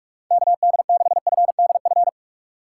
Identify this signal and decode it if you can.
Morse code, heard as a keyed tone, about 46 wpm, 700 Hz. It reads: KD6FDF
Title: USA Call Signs - 6 Characters - 3x Character Spacing - 45wpm